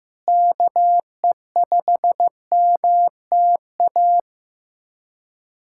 Morse code, heard as a keyed tone, about 15 wpm, 700 Hz. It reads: KE5MTA
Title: USA Callsign DE Callsign 15wpm